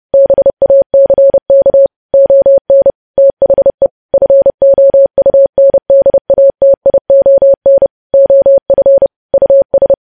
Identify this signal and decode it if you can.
Morse code, heard as a keyed tone, about 30 wpm, 550 Hz. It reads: BACK ON THE FOUNDATION OF US
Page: news headlines in morse code at 30 wpm